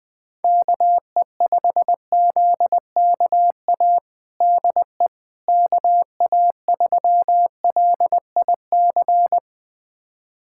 Morse code, heard as a keyed tone, about 20 wpm, 700 Hz. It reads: KE5ZKA DE KA3LIC
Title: USA Callsign DE Callsign 20wpm